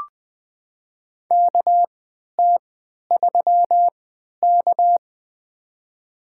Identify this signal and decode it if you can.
Morse code, heard as a keyed tone, about 20 wpm, spaced as for 11 wpm, 700 Hz. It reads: KT3K
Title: USA Call Signs - 4 Characters - 3x Character Spacing - 20wpm